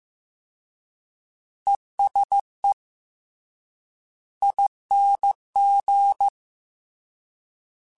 Morse code, heard as a keyed tone, about 15 wpm, 800 Hz. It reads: ESE ING
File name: Short Runs / Short Runs-mp3s / ESE ING @13